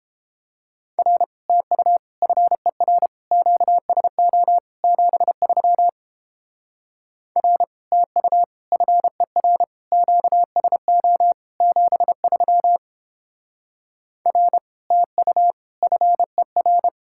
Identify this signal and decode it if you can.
Morse code, heard as a keyed tone, about 31 wpm, 700 Hz. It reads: R TU FER QSO 73 R TU FER QSO 73 R TU FER